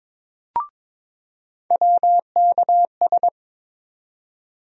Morse code, heard as a keyed tone, about 22 wpm, 700 Hz. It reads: WKS
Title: USA Call Sign Postfixes - 3 Letters 22wpm